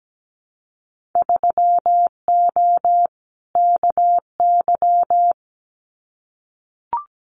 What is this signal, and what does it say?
Morse code, heard as a keyed tone, about 17 wpm, 700 Hz. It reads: 3O KY
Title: Winter Field Day Exchanges 17wpm